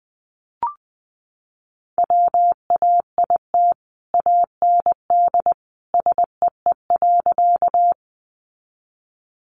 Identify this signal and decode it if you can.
Morse code, heard as a keyed tone, about 20 wpm, 700 Hz. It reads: WAIT AND SEE.
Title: Sentences from Top 500 Words 20wpm